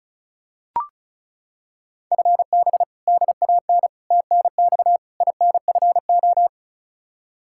Morse code, heard as a keyed tone, about 34 wpm, 700 Hz. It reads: FB DAN TNX INFO
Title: Practice QSO - Exchange 3 - More Info - 35wpm